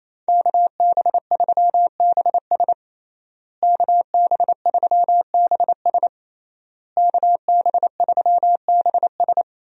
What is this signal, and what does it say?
Morse code, heard as a keyed tone, about 27 wpm, 700 Hz. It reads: KB3BS KB3BS KB3BS
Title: USA Call Signs - 5 Characters - Repeated 3x - 28wpm